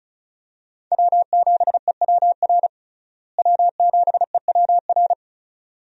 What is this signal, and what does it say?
Morse code, heard as a keyed tone, about 34 wpm, 700 Hz. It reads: W7EWR W7EWR
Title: USA Call Signs - 5 Characters - Repeated 2x - 35wpm